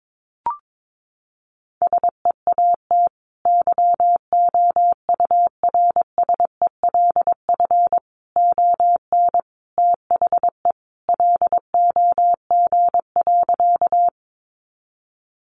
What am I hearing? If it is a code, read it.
Morse code, heard as a keyed tone, about 22 wpm, 700 Hz. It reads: SEAT YOURSELF ON THE LOG.
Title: Sentences from Top 1500 Words 22wpm